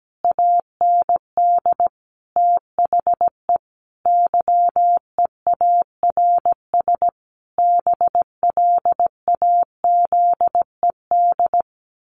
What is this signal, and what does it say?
Morse code, heard as a keyed tone, about 17 wpm, 700 Hz. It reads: AND THE YEARS BLAZED
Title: A Sound Of Thunder 17wpm